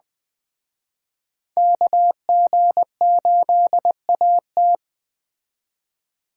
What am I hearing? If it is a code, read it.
Morse code, heard as a keyed tone, about 22 wpm, 700 Hz. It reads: KG8AT